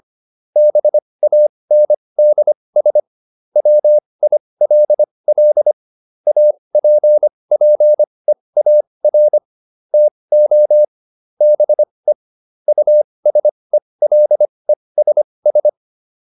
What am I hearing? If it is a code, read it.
Morse code, heard as a keyed tone, about 25 wpm, spaced as for 20 wpm, 600 Hz. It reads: BANDS WILL APPEAR TO BE USELESS